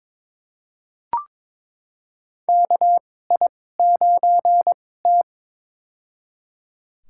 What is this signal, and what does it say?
Morse code, heard as a keyed tone, about 22 wpm, spaced as for 16 wpm, 700 Hz. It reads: KI9T
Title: USA Call Signs - 4 Characters - 2x Character Spacing - 22wpm